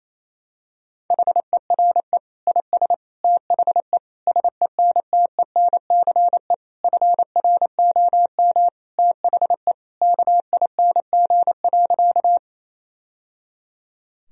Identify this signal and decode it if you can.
Morse code, heard as a keyed tone, about 28 wpm, 700 Hz. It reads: HERE IS THE SENTENCE FROM THE KING.
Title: Sentences from Top 1000 Words - Review Part 1 - 28wpm